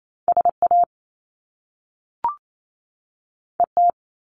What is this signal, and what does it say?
Morse code, heard as a keyed tone, about 28 wpm, 700 Hz. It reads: SA ET
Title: Top Two Letter Combinations in Words 28wpm